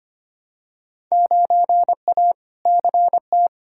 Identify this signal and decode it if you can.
Morse code, heard as a keyed tone, about 25 wpm, 700 Hz. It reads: 9A CT
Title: Field Day Exchanges Encore - 25wpm